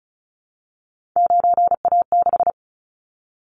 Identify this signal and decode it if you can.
Morse code, heard as a keyed tone, about 36 wpm, 690 Hz. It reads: 9A6
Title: Callsign Prefixes for Top 50 DXCC - 35wpm